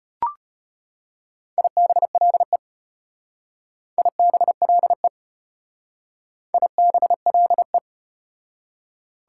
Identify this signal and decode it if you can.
Morse code, heard as a keyed tone, about 34 wpm, 700 Hz. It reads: IBLE IBLE IBLE